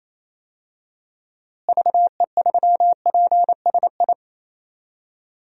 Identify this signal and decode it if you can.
Morse code, heard as a keyed tone, about 28 wpm, 700 Hz. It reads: VE3PSI
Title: DXCC-100 Call Signs - 6 Characters - Repeated 2x - 28wpm